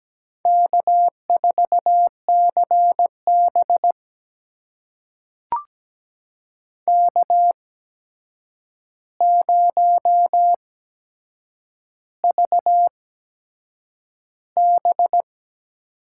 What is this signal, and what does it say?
Morse code, heard as a keyed tone, about 17 wpm, spaced as for 7 wpm, 700 Hz. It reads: K4CB K0VB
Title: USA Call Signs - 4 Characters - 8x Character Spacing - 17wpm